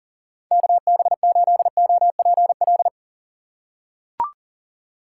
Morse code, heard as a keyed tone, about 38 wpm, 700 Hz. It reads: KB8OPL
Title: USA Call Signs - 6 Characters - 3x Character Spacing - 40wpm